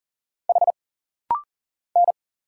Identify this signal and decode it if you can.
Morse code, heard as a keyed tone, about 40 wpm, 700 Hz. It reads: H N
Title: Single Letters - Rapid-Fire 40wpm